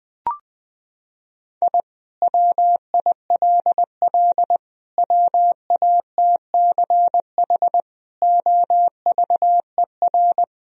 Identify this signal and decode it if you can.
Morse code, heard as a keyed tone, about 20 wpm, 700 Hz. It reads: I WILL WATCH OVER
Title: Sentences from Top 1000 Words - Review Part 1 - 20wpm